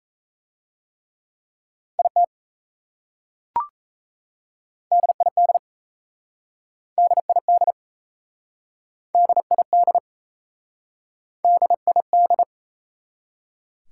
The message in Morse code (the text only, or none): IT DID DID DID DID